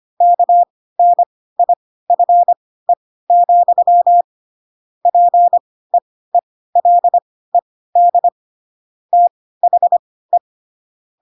Morse code, heard as a keyed tone, about 25 wpm, spaced as for 16 wpm, 700 Hz. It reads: KNIFE, PEELED THE
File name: Pinocchio / Pinocchio ch7 10 25 ik1hge